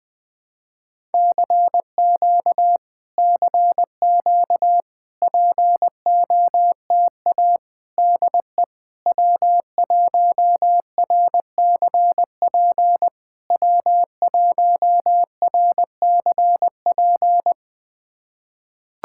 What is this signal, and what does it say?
Morse code, heard as a keyed tone, about 20 wpm, 700 Hz. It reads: CQ CQ POTA DE W1RCP W1RCP
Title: POTA QSO Practice - 20wpm